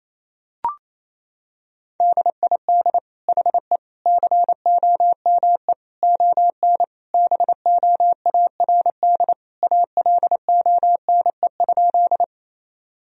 Morse code, heard as a keyed tone, about 28 wpm, 700 Hz. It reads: DID HE COME ON BOARD ALONE?
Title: Sentences from Top 1000 Words - Review Part 4 - 28wpm